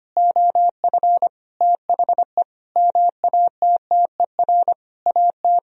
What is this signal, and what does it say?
Morse code, heard as a keyed tone, about 25 wpm, 700 Hz. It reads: OF THE MATTER AT
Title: Sentences from Top 1000 Words - Review Part 3 - 25wpm